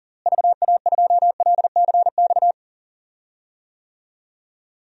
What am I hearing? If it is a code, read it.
Morse code, heard as a keyed tone, about 40 wpm, 700 Hz. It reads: VA2LCX
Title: DXCC-100 Call Signs - 6 Characters - Repeated 2x - 40wpm